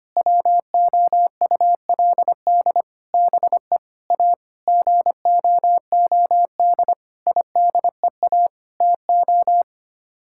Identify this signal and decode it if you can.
Morse code, heard as a keyed tone, about 25 wpm, 700 Hz. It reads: WOULD BE A GOOD IDEA TO
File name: Sets of 9 Words - 25wpm